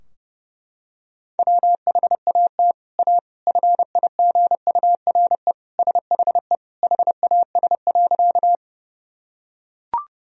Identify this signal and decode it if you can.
Morse code, heard as a keyed tone, about 32 wpm, 700 Hz. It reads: WHAT A FIGURE SHE HAS.
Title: Sentences from Top 1000 Words - Review Part 2 - 30wpm